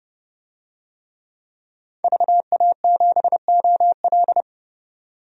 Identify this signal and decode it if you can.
Morse code, heard as a keyed tone, about 29 wpm, 700 Hz. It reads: VA7OL
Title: DXCC-100 Call Signs - 5 Characters - 30wpm